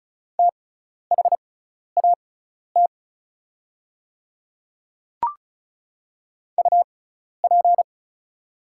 Morse code, heard as a keyed tone, about 36 wpm, spaced as for 12 wpm, 700 Hz. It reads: THAT UP
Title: 6x Character Spacing - Top 100 Words - 35wpm